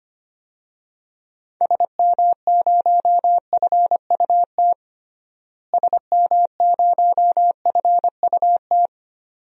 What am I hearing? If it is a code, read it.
Morse code, heard as a keyed tone, about 25 wpm, 700 Hz. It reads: SM0FUT SM0FUT
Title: DXCC-100 Call Signs - 6 Characters - Repeated 2x - 25wpm